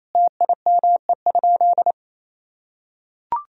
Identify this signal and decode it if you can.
Morse code, heard as a keyed tone, about 28 wpm, 700 Hz. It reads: TIME?
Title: Sentences from Top 1000 Words - Review Part 3 - 28wpm